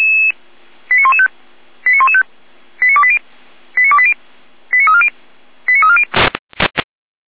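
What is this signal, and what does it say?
Five-tone selective call (ZVEI), heard as quick runs of five tones, tones of about 70 ms. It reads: 89115 89115 89280 89280 89340 89340
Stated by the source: ZVEI five-tone selective call received on an SDR.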